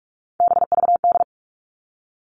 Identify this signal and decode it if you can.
Morse code, heard as a keyed tone, about 46 wpm, 700 Hz. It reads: BVD
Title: Three Challenging Letter-Number Combinations 45wpm